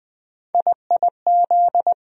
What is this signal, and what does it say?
Morse code, heard as a keyed tone, about 20 wpm, 700 Hz. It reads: IIZ
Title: USA Call Sign Postfixes - 3 Letters 20wpm